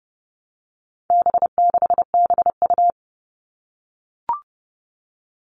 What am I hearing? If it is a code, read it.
Morse code, heard as a keyed tone, about 28 wpm, 700 Hz. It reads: B6BU
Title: All Cut Numbers - 4-Digit 30wpm